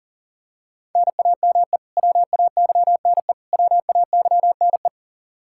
Morse code, heard as a keyed tone, about 40 wpm, 700 Hz. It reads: NAME WAYNE WAYNE
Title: Practice QSO - Exchange 1 - Beginning - 40wpm